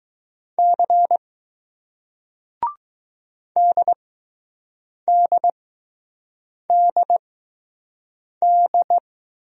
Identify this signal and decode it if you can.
Morse code, heard as a keyed tone, about 20 wpm, spaced as for 6 wpm, 700 Hz. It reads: C DDDD